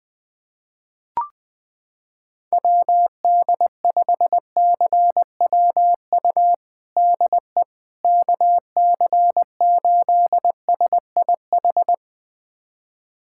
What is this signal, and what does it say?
Morse code, heard as a keyed tone, about 20 wpm, 700 Hz. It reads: WD5CWU DE KC8SIH
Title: USA Callsign DE Callsign 20wpm